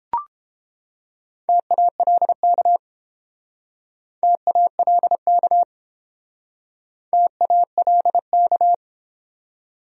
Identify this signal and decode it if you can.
Morse code, heard as a keyed tone, about 30 wpm, 700 Hz. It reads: TALK TALK TALK